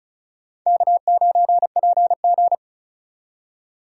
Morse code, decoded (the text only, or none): K9PG